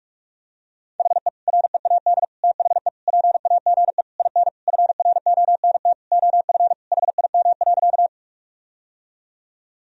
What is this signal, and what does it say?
Morse code, heard as a keyed tone, about 46 wpm, 700 Hz. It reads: HE READ THE PAGE IN FRONT OF HIM.
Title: Sentences from Top 1000 Words - Review Part 1 - 45wpm